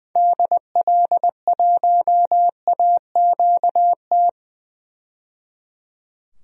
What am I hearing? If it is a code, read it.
Morse code, heard as a keyed tone, about 20 wpm, 700 Hz. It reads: DL1AQT
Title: DXCC-100 Call Signs - 6 Characters - 20wpm